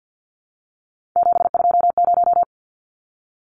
Morse code, heard as a keyed tone, about 50 wpm, 700 Hz. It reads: 720
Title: Numbers - 3-Digit 50wpm